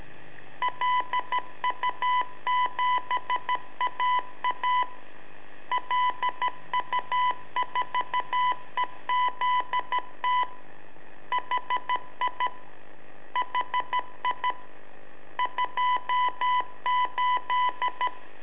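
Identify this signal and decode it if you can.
Morse code, heard as a keyed tone, about 13 wpm, 1000 Hz. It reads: LU7AA LU4EZT HI HI 28
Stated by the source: Telemetry in Morse from LU7AA and LU4EZ, AMSAT and Turma Mar del Plata, sent from a balloon.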